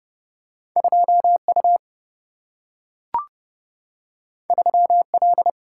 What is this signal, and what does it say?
Morse code, heard as a keyed tone, about 30 wpm, 700 Hz. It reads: 2U 3L
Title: Single-Letter Single-Number 30wpm